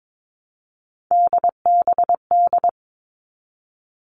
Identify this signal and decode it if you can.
Morse code, heard as a keyed tone, about 22 wpm, 700 Hz. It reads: DBD
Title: All Cut Numbers - 3-Digit 22wpm